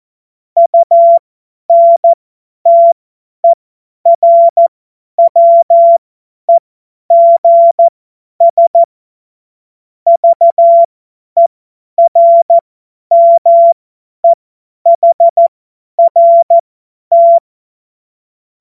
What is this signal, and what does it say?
Morse code, heard as a keyed tone, about 14 wpm, spaced as for 10 wpm, 680 Hz. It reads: UNTERWEGS VERMEHRT
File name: Morsekurs / Witze / Witz 26. April 2022 50 F70